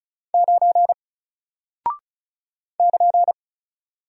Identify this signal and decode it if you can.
Morse code, heard as a keyed tone, about 36 wpm, 700 Hz. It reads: 9 <KN>
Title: Single Character (Letter:Number:Punctuation:Prosign) Alt 35wpm